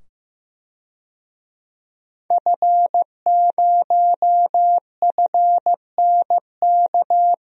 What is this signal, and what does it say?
Morse code, heard as a keyed tone, about 15 wpm, 700 Hz. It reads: F0FNK
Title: DXCC-100 Call Signs - 5 Characters - 15wpm